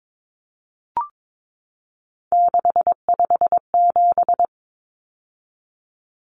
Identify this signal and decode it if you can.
Morse code, heard as a keyed tone, about 22 wpm, 700 Hz. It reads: 657
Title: Numbers - 3-Digit 22wpm